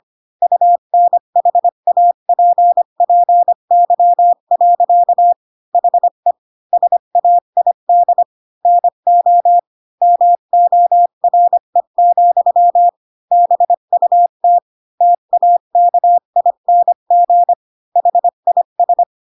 Morse code, heard as a keyed tone, about 25 wpm, 700 Hz. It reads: UNHAPPY. HE SAID NO MORE, BUT TAKING HIS
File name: Pinocchio / Pinocchio ch8 20 25 ik1hge